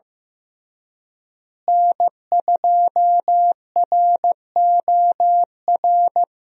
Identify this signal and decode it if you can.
Morse code, heard as a keyed tone, about 15 wpm, 700 Hz. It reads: N2ROR
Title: USA Call Signs - 5 Characters - Repeated 2x - 15wpm